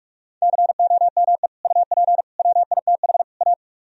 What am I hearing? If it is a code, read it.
Morse code, heard as a keyed tone, about 44 wpm, 700 Hz. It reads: COME UP WITH A